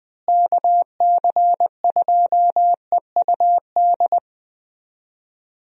Morse code, heard as a keyed tone, about 20 wpm, 700 Hz. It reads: KC2EUD
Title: USA Call Signs - 6 Characters - Repeated 2x - 20wpm